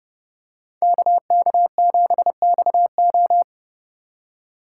Morse code, heard as a keyed tone, about 29 wpm, 700 Hz. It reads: KK7XO